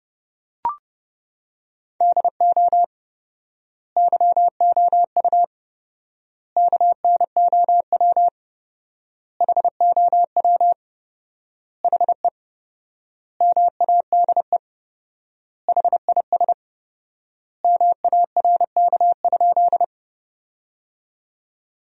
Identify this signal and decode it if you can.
Morse code, heard as a keyed tone, about 29 wpm, 700 Hz. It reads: DO YOU KNOW HOW HE MADE HIS MARK?
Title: 4x Word-Spacing - Sentences from Top 300 Words - 30wpm